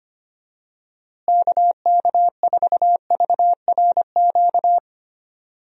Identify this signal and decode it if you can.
Morse code, heard as a keyed tone, about 25 wpm, 700 Hz. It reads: KK4VRQ
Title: USA Call Signs - 6 Characters - Repeated 3x - 25wpm